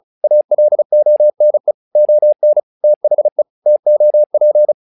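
Morse code, heard as a keyed tone, about 36 wpm, 600 Hz. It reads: ALONE ON THE TOP
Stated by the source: Morse practice text "Tower Trouble" sent at 35 WPM, from the SKCC - Straight Key Century Club learning center audio files.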